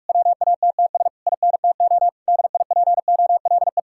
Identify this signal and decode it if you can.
Morse code, heard as a keyed tone, about 56 wpm, 700 Hz. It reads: WATTS INTO DIPOLE